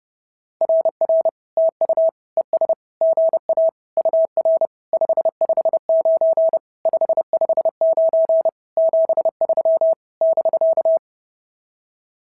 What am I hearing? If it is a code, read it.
Morse code, heard as a keyed tone, about 30 wpm, 650 Hz. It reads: RR TU ES GA UR 559 559 73 <BK>
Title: SOTA QSO Practice 30wpm